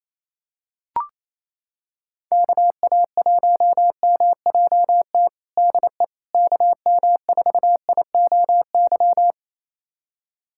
Morse code, heard as a keyed tone, about 28 wpm, 700 Hz. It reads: KA1MJT DE KM4IOY
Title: USA Callsign DE Callsign - 6 Characters - 28wpm